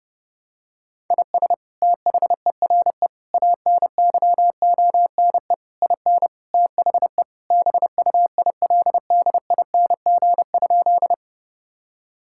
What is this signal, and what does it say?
Morse code, heard as a keyed tone, about 32 wpm, 700 Hz. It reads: IS THERE ANYONE IN THE BUILDING?
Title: Sentences from Top 1000 Words - Review Part 5 - 30wpm